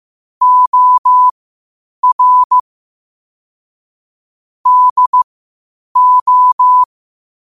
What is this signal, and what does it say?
Morse code, heard as a keyed tone, about 15 wpm, spaced as for 8 wpm, 1000 Hz. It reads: OR DO
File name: qotdfiles / 8wpm